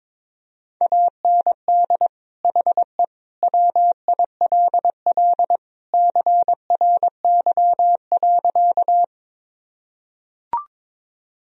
Morse code, heard as a keyed tone, about 22 wpm, 700 Hz. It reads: AND HE WILL CRY.
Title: Sentences from Top 1000 Words - Review Part 2 - 22wpm